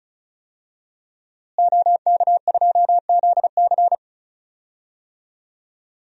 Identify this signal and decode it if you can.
Morse code, heard as a keyed tone, about 36 wpm, 700 Hz. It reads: OK2ZC
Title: DXCC-100 Call Signs - 5 Characters - Repeated 2x - 35wpm